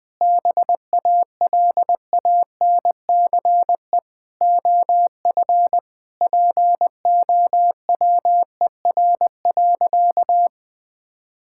Morse code, heard as a keyed tone, about 20 wpm, 700 Hz. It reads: BALANCE OF POWER.
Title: Sentences from Top 1000 Words - Review Part 6 - 20wpm